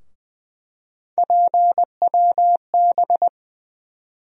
Morse code, heard as a keyed tone, about 20 wpm, 700 Hz. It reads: PWB
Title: USA Call Sign Postfixes - 3 Letters 20wpm